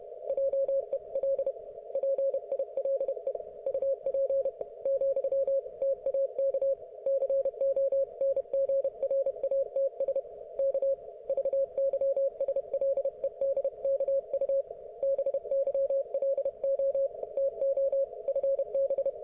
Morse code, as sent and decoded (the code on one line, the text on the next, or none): .--- . .-.. / .--. .. .-.. / ..- .--. / --..-- / - .- -.- / -.-. --- -. --. .-. .- - ... / -.- / ...- -.-- ... .-.. -.. -.- ..- / -... -.-- .-.. --- / - --- / ..-. -...
JEL PILE UP , TAK CONGRATS K VYSLEDKU BYLO TO FB